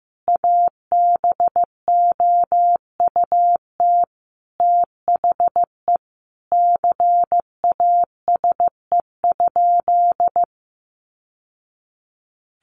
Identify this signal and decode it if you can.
Morse code, heard as a keyed tone, about 15 wpm, 700 Hz. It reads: ABOUT THE CASE?
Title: Sentences from Top 1000 Words 15wpm